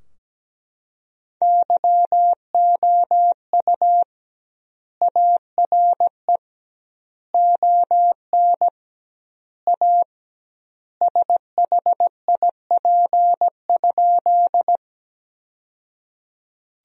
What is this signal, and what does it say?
Morse code, heard as a keyed tone, about 17 wpm, 700 Hz. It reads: YOU ARE ON A SHIP?
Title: 2x Word-Spacing - Sentences from Top 400 Words - 17wpm